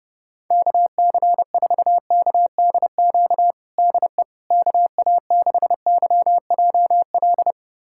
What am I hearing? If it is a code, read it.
Morse code, heard as a keyed tone, about 30 wpm, 700 Hz. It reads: KC4KDQ DE KA6YJL
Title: USA Callsign DE Callsign - 6 Characters - 30wpm